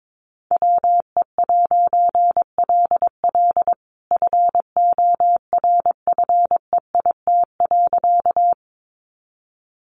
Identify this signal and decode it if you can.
Morse code, heard as a keyed tone, about 22 wpm, 700 Hz. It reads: WE'LL FORFEIT.
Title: A Sound Of Thunder 22wpm